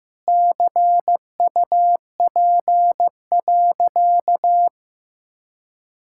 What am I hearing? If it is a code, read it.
Morse code, heard as a keyed tone, about 15 wpm, 700 Hz. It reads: CUP.
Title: Sentences from Top 1600 Words 15wpm